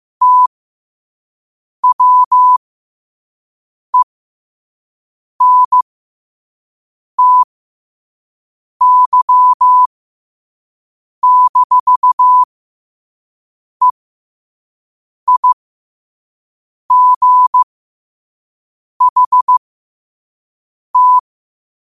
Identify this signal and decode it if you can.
Morse code, heard as a keyed tone, about 15 wpm, spaced as for 5 wpm, 1000 Hz. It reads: TWENTY-EIGHT
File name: qotdfiles / 5wpm